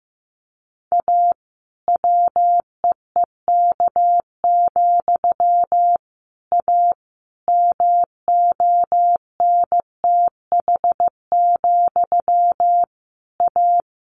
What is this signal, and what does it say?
Morse code, heard as a keyed tone, about 15 wpm, 700 Hz. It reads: A WEEK, A MONTH, A
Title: A Sound Of Thunder 15wpm